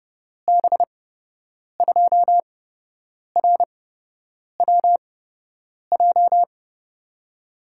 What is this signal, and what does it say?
Morse code, heard as a keyed tone, about 30 wpm, spaced as for 8 wpm, 700 Hz. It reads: B2RWJ
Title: USA Call Signs - 6 Characters - 8x Character Spacing - 30wpm